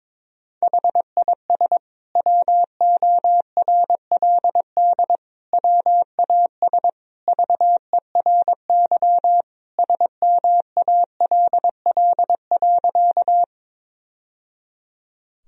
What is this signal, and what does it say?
Morse code, heard as a keyed tone, about 22 wpm, 700 Hz. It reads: HIS WORLD WAS VERY SMALL.
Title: Sentences from Top 1000 Words - Review Part 1 - 22wpm